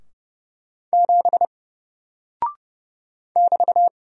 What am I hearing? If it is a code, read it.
Morse code, heard as a keyed tone, about 31 wpm, 700 Hz. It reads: 7 <BT>
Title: Single Character (Letter:Number:Punctuation:Prosign) 30wpm